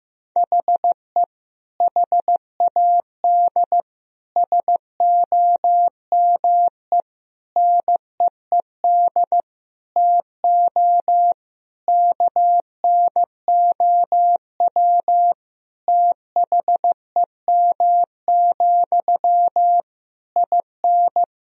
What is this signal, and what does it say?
Morse code, heard as a keyed tone, about 15 wpm, 700 Hz. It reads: HE HAD SOME NEED TO KNOW THEM, IN
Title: Sentences from Top 1000 Words - Review Part 3 - 15wpm